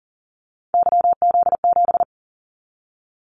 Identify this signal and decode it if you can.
Morse code, heard as a keyed tone, about 40 wpm, 700 Hz. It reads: YZ7
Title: Callsign Prefixes for Top 25 DXCC - Repeated 3x - 40wpm